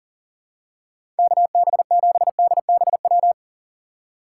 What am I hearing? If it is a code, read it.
Morse code, heard as a keyed tone, about 40 wpm, 700 Hz. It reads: KB7DBW
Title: USA Call Signs - 6 Characters - 4x Character Spacing - 40wpm